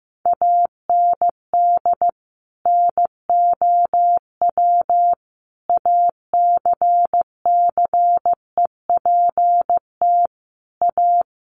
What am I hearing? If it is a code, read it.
Morse code, heard as a keyed tone, about 15 wpm, 700 Hz. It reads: AND NOW ACCEPT A
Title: Sentences from Top 1300 Words 15wpm